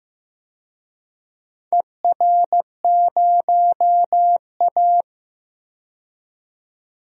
Morse code, heard as a keyed tone, about 15 wpm, 700 Hz. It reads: ER0A
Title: DXCC-100 Call Signs - 4 Characters - 15wpm